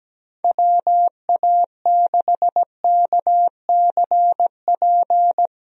Morse code, heard as a keyed tone, about 17 wpm, 700 Hz. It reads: WA6KCP
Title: USA Call Signs - 6 Characters - 17wpm